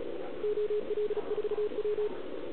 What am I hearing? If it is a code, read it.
Morse code, heard as a keyed tone, about 33 wpm, 400 Hz, with radio noise. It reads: OL4W